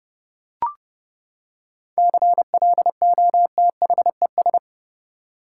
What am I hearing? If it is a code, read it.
Morse code, heard as a keyed tone, about 30 wpm, 700 Hz. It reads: CLOTHES
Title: Seven Letter Words - 30wpm